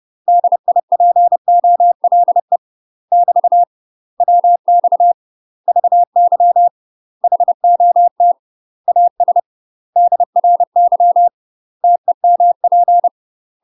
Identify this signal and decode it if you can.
Morse code, heard as a keyed tone, about 30 wpm, 700 Hz. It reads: DIPOLE <BT> WX VY HOT AS DRY TEMP
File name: UK MASTER QSOs TRAINING / 30WPM / UK MASTER QSOs 0016